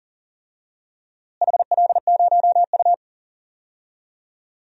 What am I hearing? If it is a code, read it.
Morse code, heard as a keyed tone, about 37 wpm, 700 Hz. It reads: HL0U
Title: DXCC-100 Call Signs - 4 Characters - Repeated 2x - 40wpm